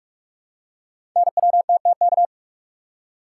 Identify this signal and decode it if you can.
Morse code, heard as a keyed tone, about 44 wpm, 690 Hz. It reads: NWTTK
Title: Code Groups - 5 Letters - 45wpm